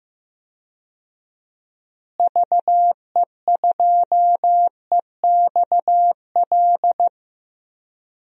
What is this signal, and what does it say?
Morse code, heard as a keyed tone, about 15 wpm, 700 Hz. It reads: VE2EXL
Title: DXCC-100 Call Signs - 6 Characters - Repeated 3x - 15wpm